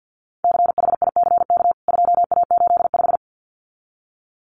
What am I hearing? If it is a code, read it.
Morse code, heard as a keyed tone, about 49 wpm, 700 Hz. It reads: CHICK 2A85
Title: CWT Exchange 50wpm